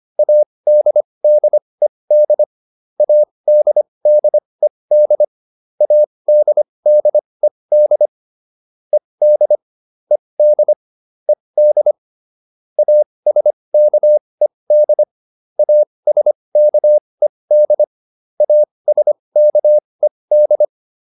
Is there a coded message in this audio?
Yes: Morse code from 0.2 to 20.6 s, heard as a keyed tone, about 25 wpm, spaced as for 20 wpm, 600 Hz. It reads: ADDED ADDED ADDED ED ED ED ASKED ASKED ASKED